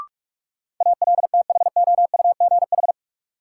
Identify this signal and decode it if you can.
Morse code, heard as a keyed tone, about 41 wpm, 700 Hz. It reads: ALTHOUGH
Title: Sentences from Top 600 Words - New Word Drill - 45wpm